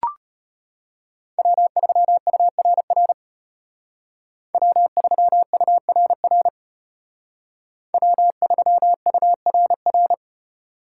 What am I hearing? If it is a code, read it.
Morse code, heard as a keyed tone, about 32 wpm, 700 Hz. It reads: W3URR W3URR W3URR